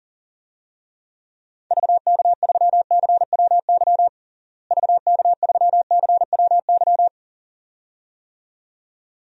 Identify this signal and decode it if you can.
Morse code, heard as a keyed tone, about 40 wpm, 700 Hz. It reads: VK3CWY VK3CWY
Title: DXCC-100 Call Signs - 6 Characters - Repeated 2x - 40wpm